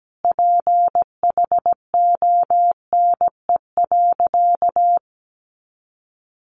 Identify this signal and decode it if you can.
Morse code, heard as a keyed tone, about 17 wpm, 700 Hz. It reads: PHONE.
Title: Sentences from Top 1100 Words 17wpm